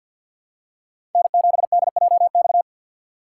Morse code, heard as a keyed tone, about 48 wpm, 700 Hz. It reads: N7DJX
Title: USA Call Signs - 5 Characters - 6x Character Spacing - 50wpm